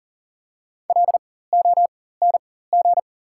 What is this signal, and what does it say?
Morse code, heard as a keyed tone, about 40 wpm, spaced as for 19 wpm, 700 Hz. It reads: LONG
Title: 4x Character Spacing - Top 500 Words - 40wpm